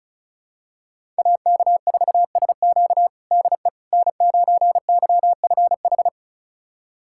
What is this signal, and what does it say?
Morse code, heard as a keyed tone, about 36 wpm, 700 Hz. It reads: AK4SQ DE N9YFH
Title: USA Callsign DE Callsign - 5 Characters - 35wpm